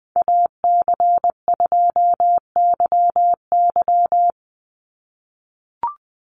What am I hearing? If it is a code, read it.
Morse code, heard as a keyed tone, about 20 wpm, 700 Hz. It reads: AC2YY
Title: USA Call Signs 20wpm